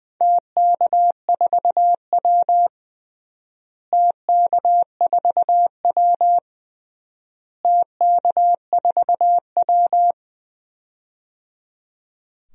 Morse code, heard as a keyed tone, about 21 wpm, 700 Hz. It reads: TK4W TK4W TK4W
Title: DXCC-100 Call Signs - 4 Characters - Repeated 3x - 20wpm